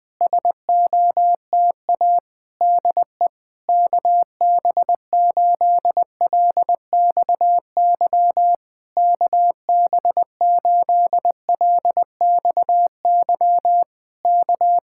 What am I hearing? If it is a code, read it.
Morse code, heard as a keyed tone, about 20 wpm, 700 Hz. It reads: SOTA DE KB8LXY KB8LXY K
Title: Speed-Racing Common QSO Phrases - 20wpm